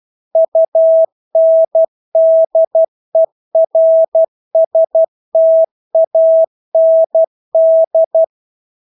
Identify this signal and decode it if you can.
Morse code, heard as a keyed tone, about 12 wpm, 650 Hz. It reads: UNDERSTAND
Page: Engelske ord 12 wpm